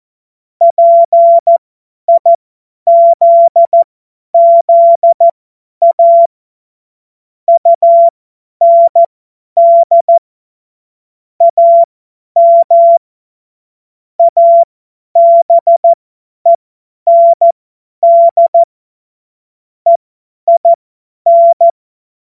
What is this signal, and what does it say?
Morse code, heard as a keyed tone, about 14 wpm, spaced as for 10 wpm, 680 Hz. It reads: PIZZA UND AM ABEND EIN
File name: Morsekurs / Witze / Witz 26 März 2024 50 F70